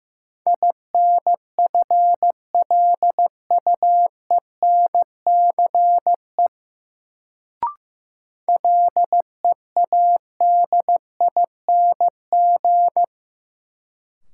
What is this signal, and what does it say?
Morse code, heard as a keyed tone, about 15 wpm, 700 Hz. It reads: INFLUENCE LEADING
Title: Sentences from Top 1100 Words - New Word Drill - 15wpm